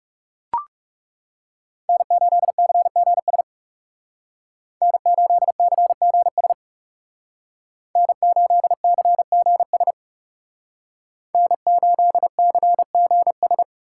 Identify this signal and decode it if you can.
Morse code, heard as a keyed tone, about 40 wpm, 700 Hz. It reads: N8CGS N8CGS N8CGS N8CGS